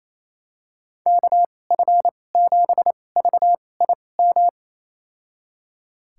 Morse code, heard as a keyed tone, about 27 wpm, spaced as for 20 wpm, 700 Hz. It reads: KF7VIM
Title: USA Call Signs - 2x Character Spacing - 28wpm